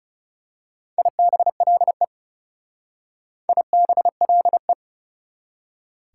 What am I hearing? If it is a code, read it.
Morse code, heard as a keyed tone, about 34 wpm, 690 Hz. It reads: IBLE IBLE
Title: Speed-Racing Prefixes and Suffixes 30wpm